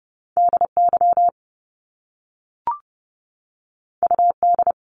Morse code, heard as a keyed tone, about 30 wpm, 700 Hz. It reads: DY UD
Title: Top Two Letter Combinations in Words 30wpm